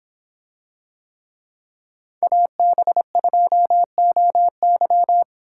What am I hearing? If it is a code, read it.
Morse code, heard as a keyed tone, about 24 wpm, 700 Hz. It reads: AB2OY